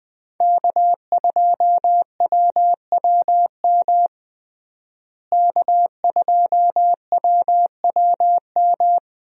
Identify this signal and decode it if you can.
Morse code, heard as a keyed tone, about 20 wpm, 700 Hz. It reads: K2WWM K2WWM
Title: USA Call Signs - 5 Characters - Repeated 2x - 20wpm